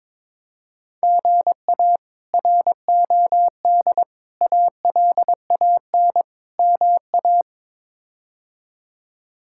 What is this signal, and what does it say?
Morse code, heard as a keyed tone, about 21 wpm, 700 Hz. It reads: GA ROD ALAN MA
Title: K1USN SST QSO Practice - 22wpm